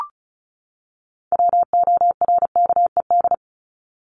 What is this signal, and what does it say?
Morse code, heard as a keyed tone, about 34 wpm, 700 Hz. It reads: WORKED
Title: ED Suffix - Top 100 Words with Base Words 35wpm